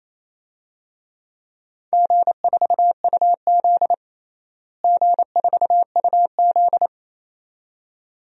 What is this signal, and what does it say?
Morse code, heard as a keyed tone, about 28 wpm, 700 Hz. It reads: G4UZ G4UZ
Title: DXCC-100 Call Signs - 4 Characters - Repeated 2x - 28wpm